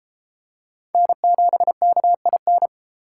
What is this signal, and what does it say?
Morse code, heard as a keyed tone, about 30 wpm, 700 Hz. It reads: N7KIN